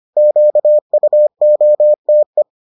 Morse code, heard as a keyed tone, about 25 wpm, 600 Hz. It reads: QUOTE